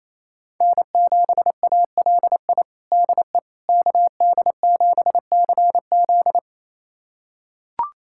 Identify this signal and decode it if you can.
Morse code, heard as a keyed tone, about 28 wpm, 700 Hz. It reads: N7ALI DE KD7CZ
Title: USA Callsign DE Callsign - 5 Characters - 28wpm